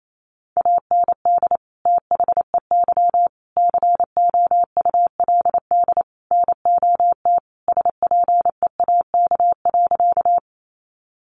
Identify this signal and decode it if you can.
Morse code, heard as a keyed tone, about 28 wpm, 700 Hz. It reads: AND THEY COULD NOT SPEAK.
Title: Sentences from Top 1100 Words 28wpm